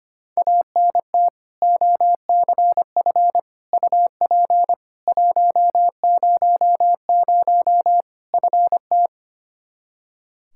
Morse code, heard as a keyed tone, about 25 wpm, 700 Hz. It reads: ANT OCF UP 100 FT
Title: K1USN SST QSO Practice 25wpm